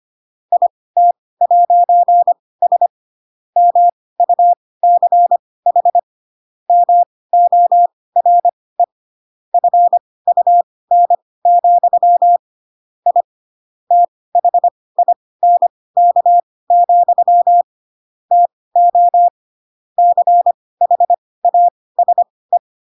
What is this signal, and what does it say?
Morse code, heard as a keyed tone, about 25 wpm, spaced as for 18 wpm, 700 Hz. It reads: IT'S MUCH MORE FUN, I THINK, TO CHASE
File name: Pinocchio / Pinocchio ch4 12 25 ik1hge